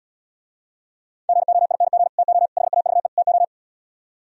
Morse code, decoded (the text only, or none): GOING TO HAVE TO